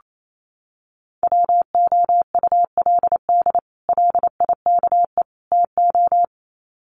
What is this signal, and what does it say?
Morse code, heard as a keyed tone, about 28 wpm, 700 Hz. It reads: WOULD LIKE TO
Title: Sets of 3 Words 28wpm